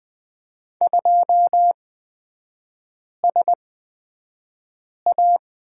Morse code, heard as a keyed tone, about 20 wpm, spaced as for 5 wpm, 700 Hz. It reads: 2SA
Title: 255 Lesson 41e - Period - Three Characters 20wpm